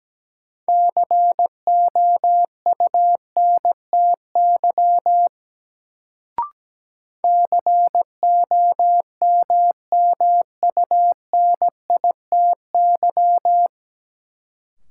Morse code, heard as a keyed tone, about 17 wpm, 700 Hz. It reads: COUNTY COMMUNITY
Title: Sentences from Top 1200 Words - New Word Drill Alt - 17wpm